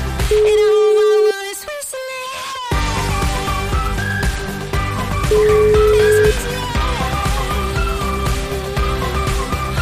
Ringback tone (Europe, 425 Hz), with noise.